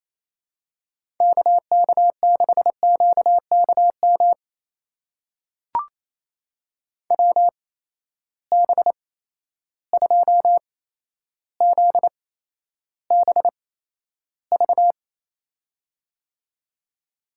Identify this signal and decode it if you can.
Morse code, heard as a keyed tone, about 28 wpm, spaced as for 12 wpm, 700 Hz. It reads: KK6QKM WB2ZBV
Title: USA Call Signs - 8x Character Spacing - 28wpm